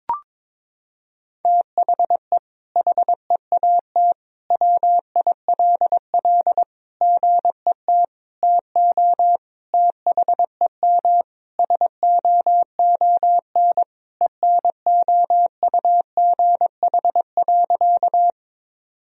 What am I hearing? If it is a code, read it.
Morse code, heard as a keyed tone, about 22 wpm, 700 Hz. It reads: THE HEAT WILL GET TO THEM SOON ENOUGH.
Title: Sentences from Top 1000 Words - Review Part 3 - 22wpm